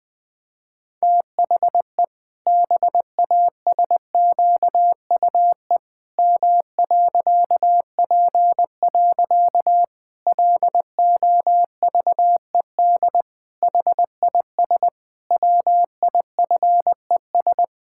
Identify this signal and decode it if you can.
Morse code, heard as a keyed tone, about 20 wpm, 700 Hz. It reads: THE BASQUE M.P. LOVED HIS WIFES
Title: Pangram Challenge v2 - 20wpm